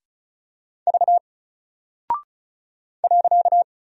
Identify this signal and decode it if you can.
Morse code, heard as a keyed tone, about 36 wpm, 700 Hz. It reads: V .
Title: Single Character (Letter:Number:Punctuation:Prosign) Alt 35wpm